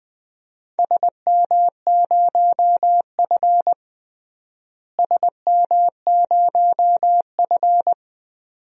Morse code, heard as a keyed tone, about 20 wpm, 700 Hz. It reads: SM0F SM0F
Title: DXCC-100 Call Signs - 4 Characters - Repeated 2x - 20wpm